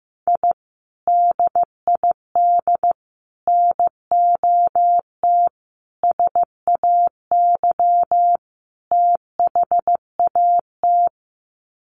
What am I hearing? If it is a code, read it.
Morse code, heard as a keyed tone, about 15 wpm, 700 Hz. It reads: I DID NOT SAY THAT
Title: Sets of 5 Words Encore2 15wpm